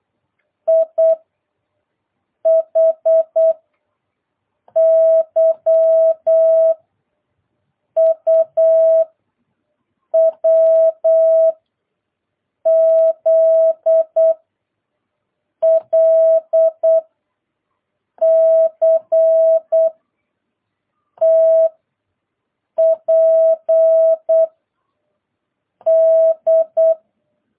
Morse code, heard as a keyed tone, about 8 wpm, 660 Hz. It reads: I H Y U W Z L C T P D